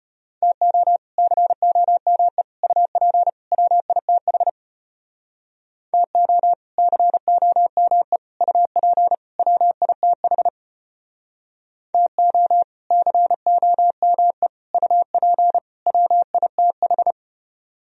Morse code, heard as a keyed tone, about 34 wpm, 700 Hz. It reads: TO COME UP WITH TO COME UP WITH TO COME UP WITH